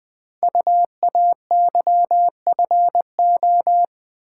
Morse code, heard as a keyed tone, about 20 wpm, 700 Hz. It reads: UAYFO
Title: Code Groups - 5 Letters - 20wpm